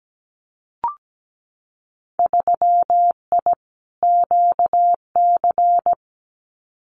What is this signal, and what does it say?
Morse code, heard as a keyed tone, about 17 wpm, 700 Hz. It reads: 3I QC
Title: Winter Field Day Exchanges Encore 17wpm